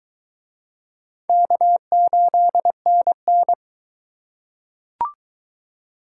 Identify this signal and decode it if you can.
Morse code, heard as a keyed tone, about 21 wpm, 700 Hz. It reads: K8NN